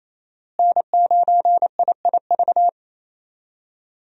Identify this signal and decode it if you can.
Morse code, heard as a keyed tone, about 28 wpm, 700 Hz. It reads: N9IIV
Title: USA Call Signs - 5 Characters - Repeated 2x - 28wpm